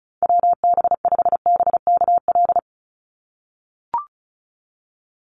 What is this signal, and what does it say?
Morse code, heard as a keyed tone, about 34 wpm, 700 Hz. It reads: WB5BKL
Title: Call Signs - 35wpm